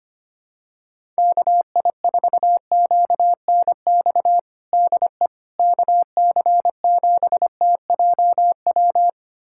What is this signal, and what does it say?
Morse code, heard as a keyed tone, about 25 wpm, 700 Hz. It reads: KI4QNX DE KC7TJW
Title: USA Callsign DE Callsign - 6 Characters - 25wpm